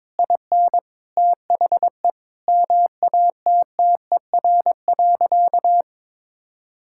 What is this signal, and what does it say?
Morse code, heard as a keyed tone, about 22 wpm, 700 Hz. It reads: IN THE MATTER.
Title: Sentences from Top 1000 Words - Review Part 6 - 22wpm